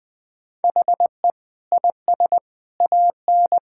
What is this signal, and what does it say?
Morse code, heard as a keyed tone, about 20 wpm, 700 Hz. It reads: HE IS AN
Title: Sentences with 5-Word Phrases - Spoken Beginning - 20wpm